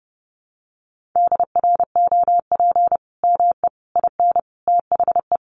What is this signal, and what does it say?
Morse code, heard as a keyed tone, about 30 wpm, 700 Hz. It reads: DROP ME IN THE
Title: Sentences from Top 1400 Words 30wpm